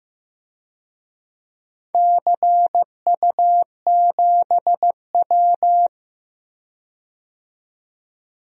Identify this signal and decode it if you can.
Morse code, heard as a keyed tone, about 15 wpm, 700 Hz. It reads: CU7W
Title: DXCC-100 Call Signs - 4 Characters - 15wpm